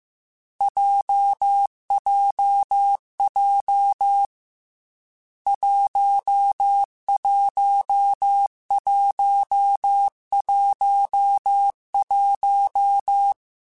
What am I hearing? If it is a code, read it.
Morse code, heard as a keyed tone, about 15 wpm, 800 Hz. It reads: JJJ 11111
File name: Difficult Combos / Difficult Combos-mp3s / AWJ1 @13